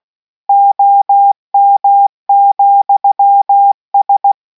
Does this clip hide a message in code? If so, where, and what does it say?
Yes: Morse code from 0.5 to 4.3 s, heard as a keyed tone, about 16 wpm, 800 Hz. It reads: OM,S